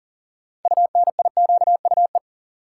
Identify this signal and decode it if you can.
Morse code, heard as a keyed tone, about 38 wpm, 700 Hz. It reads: UNIQUE
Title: Sentences from Top 1000 Words - Review Part 7 - New Word Drill - 40wpm